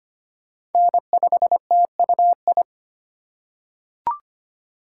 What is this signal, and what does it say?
Morse code, heard as a keyed tone, about 25 wpm, 700 Hz. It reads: N5TUI
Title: USA Call Signs - 5 Characters - 3x Character Spacing - 25wpm